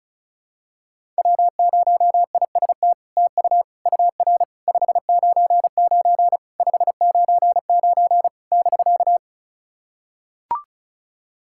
Morse code, heard as a keyed tone, about 36 wpm, 700 Hz. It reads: W0IST TU UR 599 599 <BK>
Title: SOTA QSO Practice 35wpm